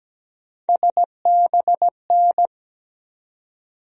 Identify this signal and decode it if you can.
Morse code, heard as a keyed tone, about 17 wpm, 700 Hz. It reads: SBN
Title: USA Call Sign Postfixes - 3 Letters 17wpm